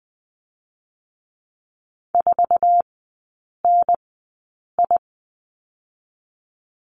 Morse code, heard as a keyed tone, about 20 wpm, spaced as for 8 wpm, 700 Hz. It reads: 4NI
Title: 107 Lesson 18e - Letter m - Three Characters 20wpm